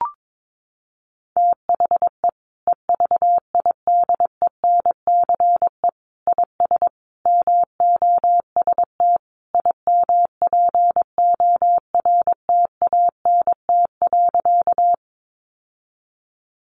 Morse code, heard as a keyed tone, about 22 wpm, 700 Hz. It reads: THE EVIDENCE IS MOST IMPORTANT.
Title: Sentences from Top 1300 Words 22wpm